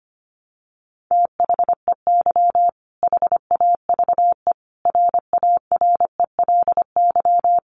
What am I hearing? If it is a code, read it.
Morse code, heard as a keyed tone, about 25 wpm, 700 Hz. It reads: THEY HAVE RARELY
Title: Sentences from Top 1400 Words 25wpm